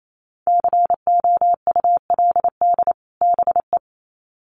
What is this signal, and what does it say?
Morse code, heard as a keyed tone, about 28 wpm, 700 Hz. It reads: COULD BE
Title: Sets of 2 Words 28wpm